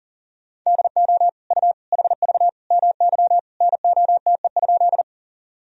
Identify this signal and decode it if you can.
Morse code, heard as a keyed tone, about 40 wpm, 700 Hz. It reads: DO U HV MY NOTE?
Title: Sentences from Top 500 Words and Abbreviations 40wpm